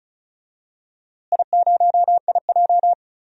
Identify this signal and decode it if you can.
Morse code, heard as a keyed tone, about 34 wpm, 700 Hz. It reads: I0IJ
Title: DXCC-100 Call Signs - 4 Characters - Repeated 2x - 35wpm